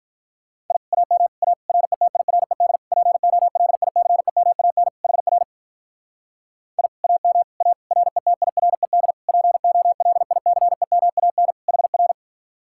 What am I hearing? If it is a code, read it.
Morse code, heard as a keyed tone, about 46 wpm, 700 Hz. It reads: I AM A RETIRED POLICEMAN HR I AM A RETIRED POLICEMAN HR